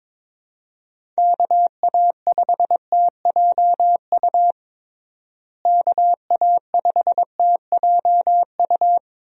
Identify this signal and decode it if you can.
Morse code, heard as a keyed tone, about 22 wpm, 700 Hz. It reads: KA5TJU KA5TJU
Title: USA Call Signs - 6 Characters - Repeated 2x - 22wpm